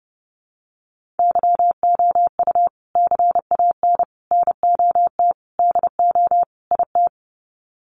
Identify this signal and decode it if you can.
Morse code, heard as a keyed tone, about 30 wpm, 700 Hz. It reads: YOU CAN NOT DO IT
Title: Sets of 5 Words Encore2 30wpm